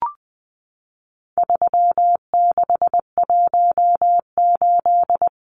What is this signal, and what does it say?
Morse code, heard as a keyed tone, about 20 wpm, 700 Hz. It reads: 3618
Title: Numbers - 4-Digit 20wpm